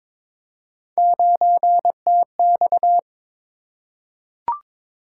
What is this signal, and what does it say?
Morse code, heard as a keyed tone, about 22 wpm, 700 Hz. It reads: 9TX
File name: practice / 22wpm / Three Letter-Number Combinations - 22wpm